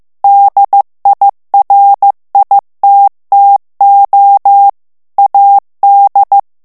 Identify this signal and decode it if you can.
Morse code, heard as a keyed tone, about 15 wpm, 800 Hz. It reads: DIRITTO AD
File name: morse mp3 / ITALIEN texte clair 2 vitesse 14 mots minute